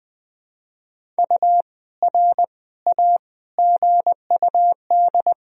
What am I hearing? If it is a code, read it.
Morse code, heard as a keyed tone, about 21 wpm, 700 Hz. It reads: U R A GUD
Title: Sentences from Top 400 Words and Abbreviations 20wpm